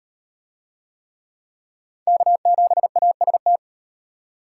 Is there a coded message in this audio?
Yes: Morse code from 2.1 to 3.6 s, heard as a keyed tone, about 36 wpm, 700 Hz. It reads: K7AST